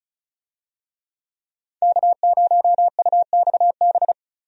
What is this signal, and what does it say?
Morse code, heard as a keyed tone, about 36 wpm, 700 Hz. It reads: K0UXB
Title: USA Call Signs - 5 Characters - Repeated 2x - 35wpm